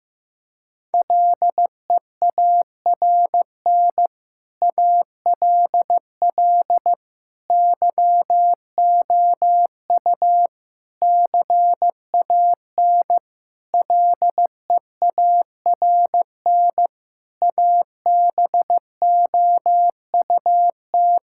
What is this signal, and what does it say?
Morse code, heard as a keyed tone, about 16 wpm, 700 Hz. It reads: LEARN ALL YOU CAN LEARN ABOUT
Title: Sentences from Top 1000 Words - Review Part 5 - 15wpm